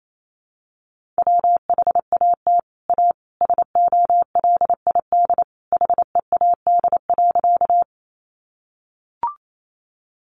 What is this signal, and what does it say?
Morse code, heard as a keyed tone, about 28 wpm, 700 Hz. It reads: WHAT A SOLID HEAD.
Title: Sentences from Top 1500 Words 28wpm